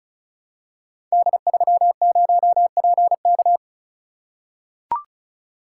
Morse code, heard as a keyed tone, about 36 wpm, 690 Hz. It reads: D30PK
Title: Code Groups - 5 Letter:Numbers - 35wpm